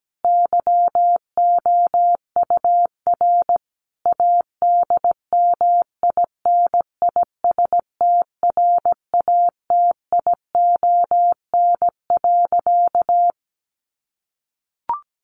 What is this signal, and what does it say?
Morse code, heard as a keyed tone, about 17 wpm, 700 Hz. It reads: YOUR ADMINISTRATION.
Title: Sentences from Top 1500 Words 17wpm